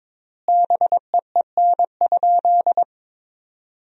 Morse code, heard as a keyed tone, about 22 wpm, 700 Hz. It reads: BEEN?
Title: 4x Word-Spacing - Sentences from Top 400 Words - 22wpm